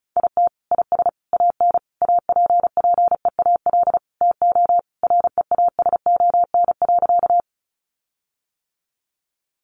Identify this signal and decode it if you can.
Morse code, heard as a keyed tone, about 34 wpm, 700 Hz. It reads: IT IS AN APPEAL TO REASON.
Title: Sentences from Top 1300 Words 35wpm